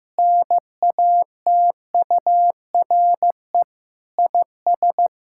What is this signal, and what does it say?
Morse code, heard as a keyed tone, about 15 wpm, 700 Hz. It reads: NATURE IS
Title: Sentences from Top 1000 Words - Review Part 5 - 15wpm